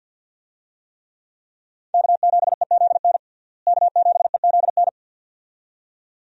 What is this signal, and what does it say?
Morse code, heard as a keyed tone, about 48 wpm, 700 Hz. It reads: K7EZN K7EZN
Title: USA Call Signs - 5 Characters - Repeated 2x - 50wpm